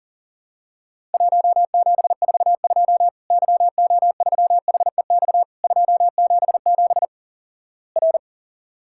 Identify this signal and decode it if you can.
Morse code, heard as a keyed tone, about 43 wpm, 700 Hz. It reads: 1742 YO3HEX 277 R
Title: Worked All Europe DX Contest - QTC Exchange - 40wpm